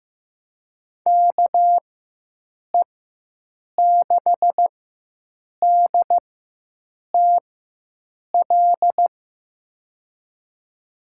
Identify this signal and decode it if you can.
Morse code, heard as a keyed tone, about 15 wpm, spaced as for 7 wpm, 700 Hz. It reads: KE6DTL
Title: USA Call Signs - 4x Character Spacing - 15wpm